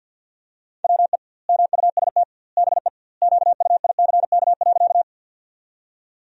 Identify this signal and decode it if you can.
Morse code, heard as a keyed tone, about 51 wpm, 700 Hz. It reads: WE MUST BE QUICK.
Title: 2x Word-Spacing - Sentences from Top 500 Words - 50wpm